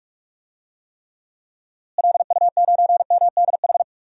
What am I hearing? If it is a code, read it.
Morse code, heard as a keyed tone, about 43 wpm, 700 Hz. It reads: LU9MDH
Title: DXCC-100 Call Signs - 6 Characters - Repeated 2x - 45wpm